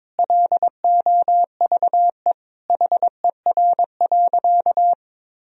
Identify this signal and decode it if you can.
Morse code, heard as a keyed tone, about 22 wpm, 700 Hz. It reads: LOVE HER.
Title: Sentences from Top 1000 Words - Review Part 4 - 22wpm